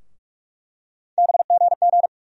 Morse code, heard as a keyed tone, about 46 wpm, 700 Hz. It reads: BGG
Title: USA Call Sign Postfixes - 3 Letters 45wpm